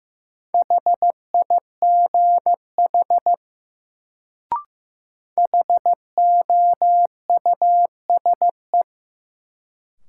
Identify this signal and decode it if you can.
Morse code, heard as a keyed tone, about 16 wpm, 700 Hz. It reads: HIGH HOUSE
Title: Sets of 2 Words - New Word Drill - 15wpm